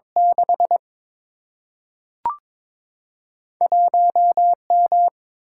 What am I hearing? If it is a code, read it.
Morse code, heard as a keyed tone, about 22 wpm, 700 Hz. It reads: 6 1M
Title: Single-Letter Single-Number 22wpm